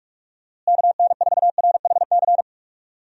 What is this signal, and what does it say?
Morse code, heard as a keyed tone, about 48 wpm, 700 Hz. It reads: KN4RHC
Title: USA Call Signs - 6 Characters - Repeated 3x - 45wpm